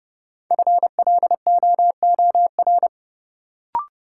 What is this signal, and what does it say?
Morse code, heard as a keyed tone, about 30 wpm, 700 Hz. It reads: FLOOR
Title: Sets of 3 Words - New Word Drill - 30wpm